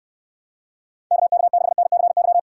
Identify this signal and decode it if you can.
Morse code, heard as a keyed tone, about 67 wpm, 700 Hz. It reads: KK6NCY